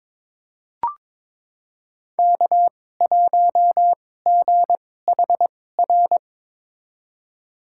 Morse code, heard as a keyed tone, about 22 wpm, spaced as for 16 wpm, 700 Hz. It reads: K1GHR
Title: USA Call Signs - 5 Characters - 2x Character Spacing - 22wpm